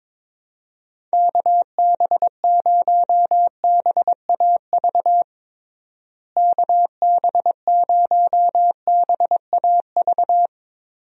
Morse code, heard as a keyed tone, about 23 wpm, 700 Hz. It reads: KB0BAV KB0BAV
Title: USA Call Signs - 6 Characters - Repeated 2x - 22wpm